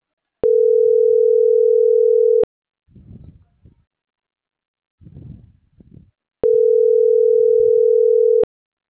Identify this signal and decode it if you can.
Ringback tone (North America), 440 + 480 Hz.
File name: CENTRAL MAERSK / out gabrielle.melendez - .28652